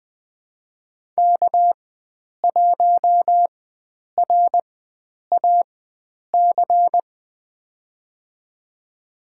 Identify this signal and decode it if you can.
Morse code, heard as a keyed tone, about 20 wpm, spaced as for 9 wpm, 700 Hz. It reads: K1RAC
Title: USA Call Signs - 5 Characters - 4x Character Spacing - 20wpm